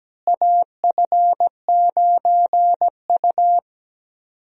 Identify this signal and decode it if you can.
Morse code, heard as a keyed tone, about 17 wpm, 700 Hz. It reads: AF9U
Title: USA Call Signs - 4 Characters - Repeated 2x - 17wpm